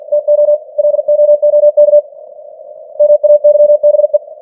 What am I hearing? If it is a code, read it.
Morse code, heard as a keyed tone, about 50 wpm, 590 Hz. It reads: TO LOOK MAYBE
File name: archived / CWBros